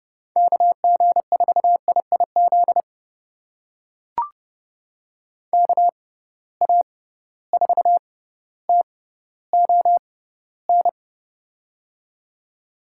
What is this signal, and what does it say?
Morse code, heard as a keyed tone, about 30 wpm, spaced as for 15 wpm, 700 Hz. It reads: KG4IIZ KA4TON
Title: USA Call Signs - 6 Characters - 6x Character Spacing - 30wpm